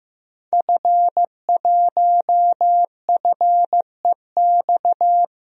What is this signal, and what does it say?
Morse code, heard as a keyed tone, about 15 wpm, 700 Hz. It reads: F1FEX
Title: DXCC-100 Call Signs - 5 Characters - Repeated 2x - 15wpm